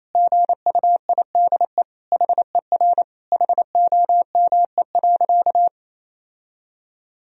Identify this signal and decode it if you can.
Morse code, heard as a keyed tone, about 28 wpm, 700 Hz. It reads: GUIDE HER HOME.
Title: Sentences from Top 1000 Words - Review Part 3 - 28wpm